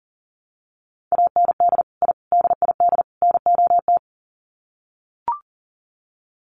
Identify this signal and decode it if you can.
Morse code, heard as a keyed tone, about 40 wpm, 700 Hz. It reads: AND I DID NOT
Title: Sets of 4 Words 40wpm